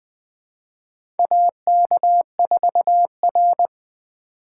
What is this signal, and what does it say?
Morse code, heard as a keyed tone, about 20 wpm, 700 Hz. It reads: AK4R
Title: USA Call Signs - 4 Characters - 4x Character Spacing - 20wpm